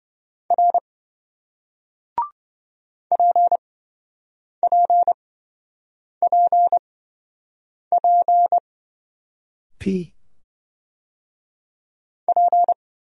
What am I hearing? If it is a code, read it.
Morse code, heard as a keyed tone, about 26 wpm, spaced as for 7 wpm, 700 Hz. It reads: R PPPP P